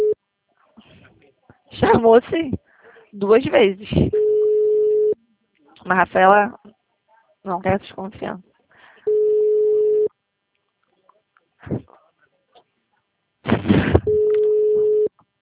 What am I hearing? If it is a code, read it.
Ringback tone (Europe, 425 Hz), with speech.